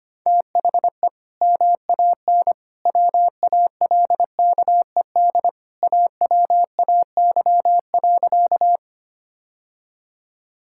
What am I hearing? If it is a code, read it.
Morse code, heard as a keyed tone, about 25 wpm, 700 Hz. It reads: THE MAN WALKED AWAY.
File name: practice / 25wpm / Sentences with 2-Word Phrases - 25wpm